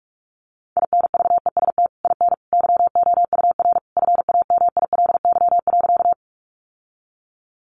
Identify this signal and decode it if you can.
Morse code, heard as a keyed tone, about 44 wpm, 700 Hz. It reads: INVEST IN YOUR FAMILY.
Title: Sentences from Top 1100 Words 45wpm